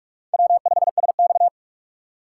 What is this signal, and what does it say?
Morse code, heard as a keyed tone, about 42 wpm, 700 Hz. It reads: W5SX